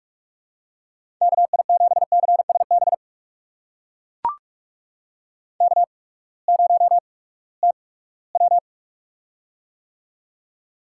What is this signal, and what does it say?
Morse code, heard as a keyed tone, about 43 wpm, spaced as for 22 wpm, 700 Hz. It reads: KI7CSB K0TW
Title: USA Call Signs - 8x Character Spacing - 45wpm